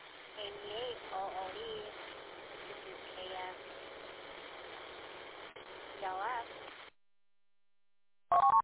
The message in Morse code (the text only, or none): none